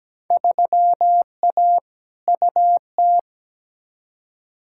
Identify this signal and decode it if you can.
Morse code, heard as a keyed tone, about 17 wpm, 700 Hz. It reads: 3A UT
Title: Field Day Exchanges - 17wpm